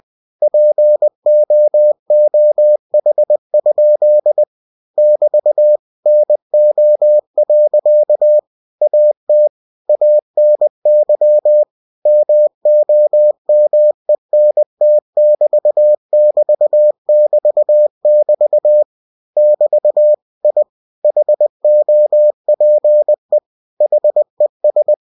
Morse code, heard as a keyed tone, about 20 wpm, 600 Hz. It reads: POOH? <BT> NO. AT ANY MOMENT<BT><BT><BT><BT> <BT> I HOPE HES
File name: cw-books / winnie-the-pooh / winnie-20wpm-page-0097